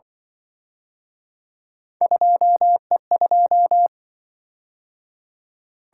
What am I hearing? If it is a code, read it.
Morse code, heard as a keyed tone, about 23 wpm, 700 Hz. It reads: 2E2
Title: Speed-Racing - Three Letter-Number Combinations - 20wpm